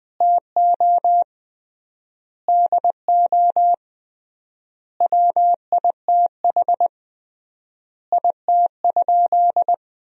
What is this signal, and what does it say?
Morse code, heard as a keyed tone, about 21 wpm, 700 Hz. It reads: TO DO WITH IT?
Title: 3x Word-Spacing - Sentences from Top 100 Words - 20wpm